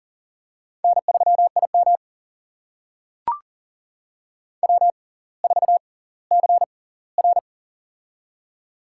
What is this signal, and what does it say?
Morse code, heard as a keyed tone, about 39 wpm, spaced as for 20 wpm, 700 Hz. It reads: N3IM W4CR
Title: USA Call Signs - 4 Characters - 6x Character Spacing - 40wpm